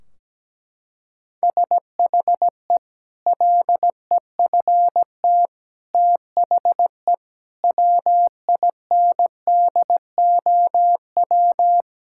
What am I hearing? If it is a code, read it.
Morse code, heard as a keyed tone, about 17 wpm, 700 Hz. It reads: SHE LEFT THE WINDOW
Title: Sentences from Top 1600 Words 17wpm